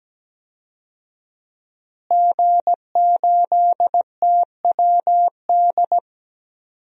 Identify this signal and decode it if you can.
Morse code, heard as a keyed tone, about 17 wpm, 700 Hz. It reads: G8TWD
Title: DXCC-100 Call Signs - 5 Characters - Repeated 2x - 17wpm